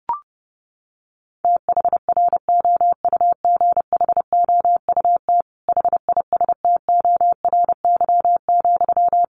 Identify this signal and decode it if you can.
Morse code, heard as a keyed tone, about 30 wpm, 700 Hz. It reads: THROUGHOUT HISTORY,
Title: Sentences from Top 800 Words 30wpm